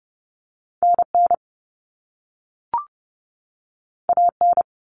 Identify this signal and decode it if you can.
Morse code, heard as a keyed tone, about 30 wpm, 700 Hz. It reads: NN AN
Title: Common Cut Numbers - 2-Digit 30wpm